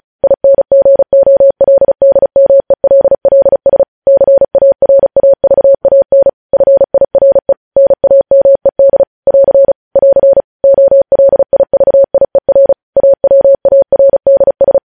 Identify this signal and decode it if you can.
Morse code, heard as a keyed tone, about 36 wpm, 550 Hz. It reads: INGOLDMELLS CARAVAN FIRE NAMED <AR> <AR> OLIVIER AWARDS
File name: podcast / News headlines-1x-35x35